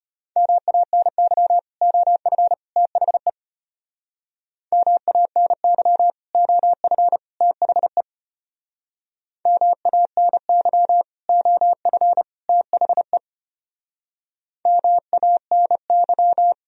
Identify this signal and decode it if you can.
Morse code, heard as a keyed tone, about 31 wpm, 700 Hz. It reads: MANY OF THE MANY OF THE MANY OF THE MANY
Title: Speed-Racing - Sets of 3 Words 25wpm